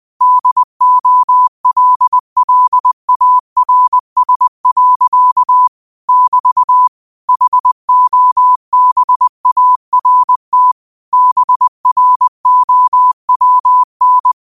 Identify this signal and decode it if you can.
Morse code, heard as a keyed tone, about 20 wpm, 1000 Hz. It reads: DOLLARS. <BT> HOBART BROWN
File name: qotdfiles / 20wpm